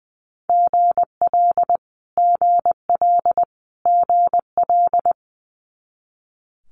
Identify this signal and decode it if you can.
Morse code, heard as a keyed tone, about 20 wpm, 700 Hz. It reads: GL GL GL
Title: Multi-letter Phonemes Repeated 3x 20wpm